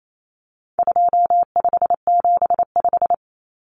Morse code, heard as a keyed tone, about 28 wpm, 700 Hz. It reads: 2575
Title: Numbers - 4-Digit 28wpm